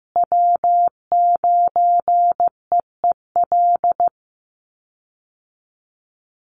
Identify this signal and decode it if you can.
Morse code, heard as a keyed tone, about 15 wpm, 700 Hz. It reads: W9EEL
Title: USA Call Signs Repeated 2x 15wpm